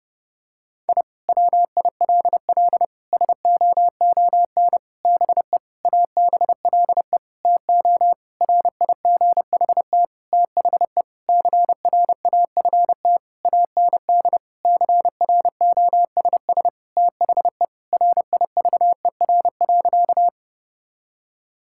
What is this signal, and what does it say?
Morse code, heard as a keyed tone, about 30 wpm, 700 Hz. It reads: I WILL SOON BE ABLE TO RIGHT THE CRAFT AND CROSS THE RIVER.
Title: Sentences from Top 1000 Words - Review Part 6 - 30wpm